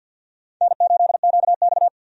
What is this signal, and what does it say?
Morse code, heard as a keyed tone, about 50 wpm, 700 Hz. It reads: N8QX
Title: USA Call Signs - 4 Characters - 50wpm